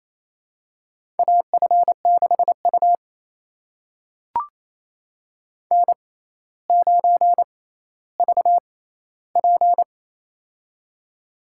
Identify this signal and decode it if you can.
Morse code, heard as a keyed tone, about 29 wpm, spaced as for 15 wpm, 700 Hz. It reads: AF6U N9VP
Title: USA Call Signs - 4 Characters - 6x Character Spacing - 28wpm